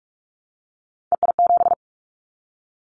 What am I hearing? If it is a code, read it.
Morse code, heard as a keyed tone, about 46 wpm, 700 Hz. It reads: EI7
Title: Callsign Prefixes for Top 50 DXCC - 45wpm